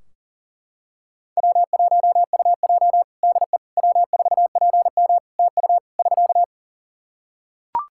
Morse code, heard as a keyed tone, about 40 wpm, 700 Hz. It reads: W1UJ DE W4PM TU <SK>
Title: K1USN SST QSO Practice 40wpm